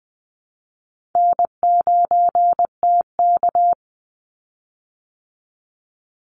Morse code, heard as a keyed tone, about 20 wpm, 700 Hz. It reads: N9TK
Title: Easy Call Signs Repeated 2x - 20wpm